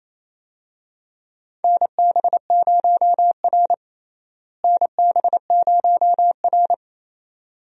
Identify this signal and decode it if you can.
Morse code, heard as a keyed tone, about 27 wpm, 700 Hz. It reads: NB0R NB0R
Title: USA Call Signs - 4 Characters - Repeated 2x - 28wpm